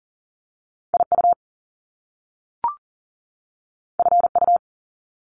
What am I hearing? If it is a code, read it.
Morse code, heard as a keyed tone, about 38 wpm, 700 Hz. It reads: IU FU
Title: Top Two Letter Combinations in Words 40wpm